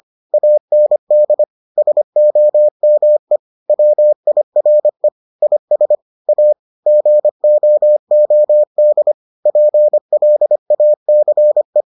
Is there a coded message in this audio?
Yes: Morse code from 0.3 to 11.8 s, heard as a keyed tone, about 25 wpm, 600 Hz. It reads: AND SOME WIRE IS A GOOD PLACE